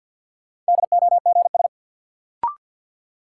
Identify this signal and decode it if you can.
Morse code, heard as a keyed tone, about 53 wpm, 700 Hz. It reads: DOGS
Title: Sentences from Top 1600 Words - New Word Drill Alt - 50wpm